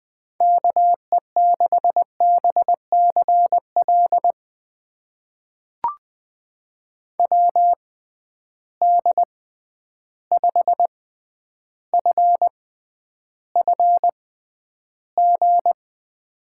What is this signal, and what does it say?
Morse code, heard as a keyed tone, about 20 wpm, spaced as for 10 wpm, 700 Hz. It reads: KE6BCL WD5FFG
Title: USA Call Signs - 6 Characters - 6x Character Spacing - 20wpm